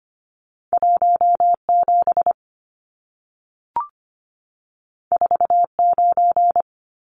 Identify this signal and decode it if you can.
Morse code, heard as a keyed tone, about 25 wpm, 700 Hz. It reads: 17 49
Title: Numbers - 2-Digit 25wpm